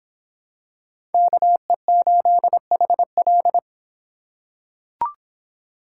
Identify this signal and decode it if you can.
Morse code, heard as a keyed tone, about 26 wpm, 700 Hz. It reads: KE8HL